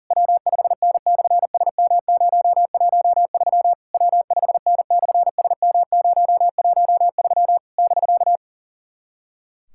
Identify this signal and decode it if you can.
Morse code, heard as a keyed tone, about 44 wpm, 700 Hz. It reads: W5N/SM013 W5N/SM013 <BK>
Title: SOTA QSO Practice 40wpm